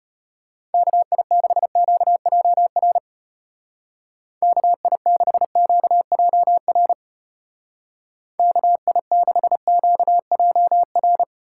Morse code, heard as a keyed tone, about 31 wpm, 700 Hz. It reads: KI6QJR KI6QJR KI6QJR